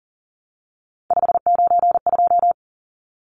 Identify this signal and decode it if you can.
Morse code, heard as a keyed tone, about 37 wpm, 700 Hz. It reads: 592
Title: Numbers - 3-Digit 40wpm